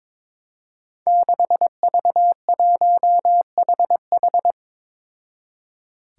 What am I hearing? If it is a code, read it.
Morse code, heard as a keyed tone, about 22 wpm, 700 Hz. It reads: 6V1HH
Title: Five Challenging Letter-Number Combinations 22wpm